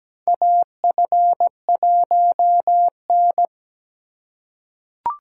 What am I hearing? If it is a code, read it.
Morse code, heard as a keyed tone, about 17 wpm, 700 Hz. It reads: AF1N
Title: USA Call Signs - 4 Characters - 4x Character Spacing - 17wpm